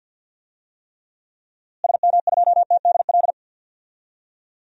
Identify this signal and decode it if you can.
Morse code, heard as a keyed tone, about 52 wpm, 700 Hz. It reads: SM2TDL
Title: DXCC-100 Call Signs - 6 Characters - Repeated 3x - 50wpm